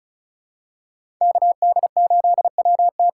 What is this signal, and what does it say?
Morse code, heard as a keyed tone, about 36 wpm, 700 Hz. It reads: KD8WT
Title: USA Call Signs - 6 Characters - 6x Character Spacing - 35wpm